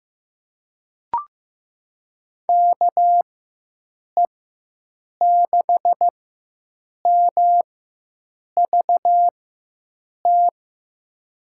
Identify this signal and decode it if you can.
Morse code, heard as a keyed tone, about 15 wpm, spaced as for 7 wpm, 700 Hz. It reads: KE6MVT
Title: USA Call Signs - 6 Characters - 4x Character Spacing - 15wpm